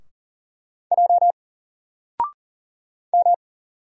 Morse code, heard as a keyed tone, about 40 wpm, 700 Hz. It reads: J M
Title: Single Character (Letter:Number:Punctuation:Prosign) Alt 40wpm